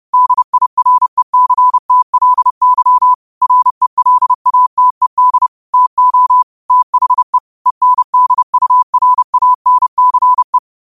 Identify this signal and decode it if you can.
Morse code, heard as a keyed tone, about 30 wpm, 1000 Hz. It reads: DIRECTLY RELATED TO THE ENDURANCE
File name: qotdfiles / 30wpm